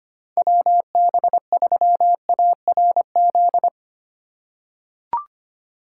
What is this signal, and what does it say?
Morse code, heard as a keyed tone, about 25 wpm, 700 Hz. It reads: WB3ARZ
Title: USA Call Signs - 6 Characters - 8x Character Spacing - 25wpm